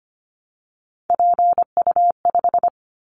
Morse code, heard as a keyed tone, about 25 wpm, 700 Hz. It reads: PU5
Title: Callsign Prefixes for Top 50 DXCC - 25wpm